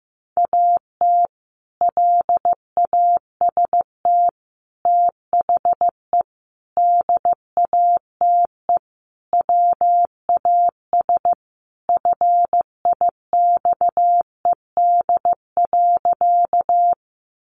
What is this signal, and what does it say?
Morse code, heard as a keyed tone, about 15 wpm, 700 Hz. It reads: AT LAST THE DATE WAS FIXED.
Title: Sentences from Top 1100 Words 15wpm